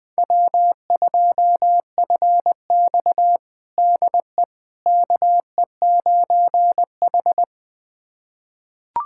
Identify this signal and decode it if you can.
Morse code, heard as a keyed tone, about 20 wpm, 700 Hz. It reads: W2FX DE KE9H
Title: USA Callsign DE Callsign - 4 Characters - 20wpm